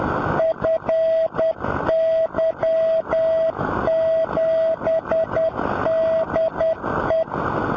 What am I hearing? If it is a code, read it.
Morse code, heard as a keyed tone, about 10 wpm, 650 Hz, with noise. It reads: FY7DE
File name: FY7DE B